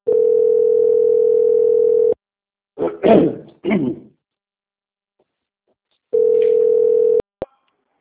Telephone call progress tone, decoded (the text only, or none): ringback tone